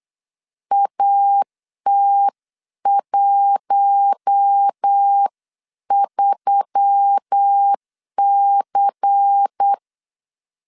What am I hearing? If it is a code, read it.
Morse code, heard as a keyed tone, about 8 wpm, 800 Hz. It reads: AT13C
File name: CW / DET / 1 9 1500C S C 8wpm 5653